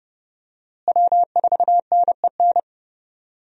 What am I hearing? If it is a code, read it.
Morse code, heard as a keyed tone, about 30 wpm, 700 Hz. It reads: W4NEN
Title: USA Call Signs - 5 Characters - Repeated 2x - 30wpm